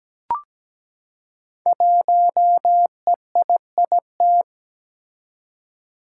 Morse code, heard as a keyed tone, about 17 wpm, 700 Hz. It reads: 1EIIT
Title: Five Challenging Letter-Number Combinations 17wpm